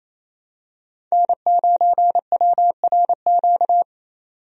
Morse code, heard as a keyed tone, about 28 wpm, 700 Hz. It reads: N9WRQ
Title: USA Call Signs - 5 Characters - 28wpm